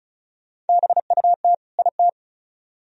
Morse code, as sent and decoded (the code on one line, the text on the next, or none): -... ..- - / .. -
BUT IT